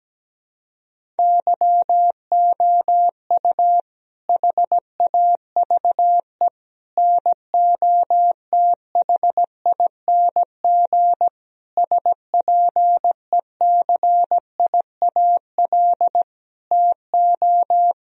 Morse code, heard as a keyed tone, about 17 wpm, 700 Hz. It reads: YOU HAVE NOTHING SPECIAL TO
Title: Sentences from Top 1000 Words - Review Part 3 - 17wpm